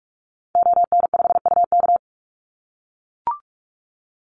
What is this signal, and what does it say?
Morse code, heard as a keyed tone, about 44 wpm, 700 Hz. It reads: ON5UK
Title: Call Signs - 45wpm